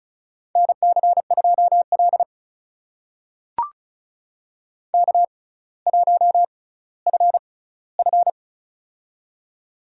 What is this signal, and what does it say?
Morse code, heard as a keyed tone, about 36 wpm, spaced as for 18 wpm, 700 Hz. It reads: NC2L K1FF
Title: USA Call Signs - 4 Characters - 6x Character Spacing - 35wpm